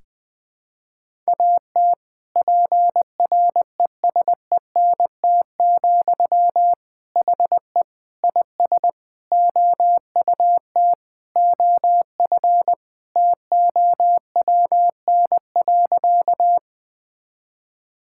Morse code, heard as a keyed tone, about 21 wpm, 700 Hz. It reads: AT PRESENT, HE IS OUT OF TOWN.
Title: Sentences from Top 1000 Words - Review Part 5 - 20wpm